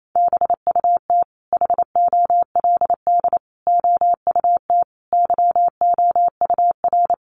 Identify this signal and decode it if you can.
Morse code, heard as a keyed tone, about 28 wpm, 700 Hz. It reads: BUT HOLD OUT YOUR
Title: Sentences from Top 1300 Words 28wpm